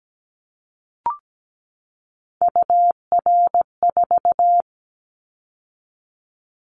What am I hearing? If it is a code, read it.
Morse code, heard as a keyed tone, about 17 wpm, 700 Hz. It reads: UR4
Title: Callsign Prefixes for Top 25 DXCC - 17wpm